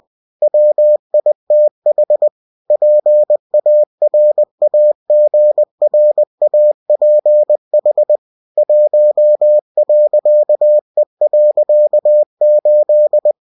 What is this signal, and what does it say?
Morse code, heard as a keyed tone, about 20 wpm, 600 Hz. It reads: WITH PARAGRAPH 1.E.8
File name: cw-books / winnie-the-pooh / winnie-20wpm-page-0116